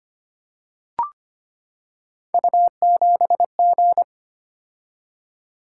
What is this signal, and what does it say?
Morse code, heard as a keyed tone, about 25 wpm, 700 Hz. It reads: U7G
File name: Three Letter-Number Combinations - 25wpm